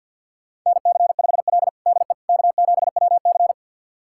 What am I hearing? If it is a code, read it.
Morse code, heard as a keyed tone, about 52 wpm, 700 Hz. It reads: NC5L DE K7WC
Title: USA Callsign DE Callsign - 4 Characters - 50wpm